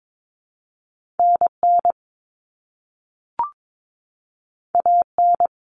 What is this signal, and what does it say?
Morse code, heard as a keyed tone, about 22 wpm, 700 Hz. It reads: NN AN
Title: Common Cut Numbers - 2-Digit 22wpm